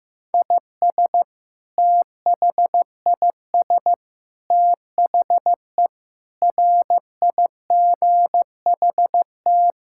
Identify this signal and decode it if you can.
Morse code, heard as a keyed tone, about 15 wpm, 700 Hz. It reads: IS THIS THE RIGHT
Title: Sentences from Top 1000 Words - Review Part 6 - 15wpm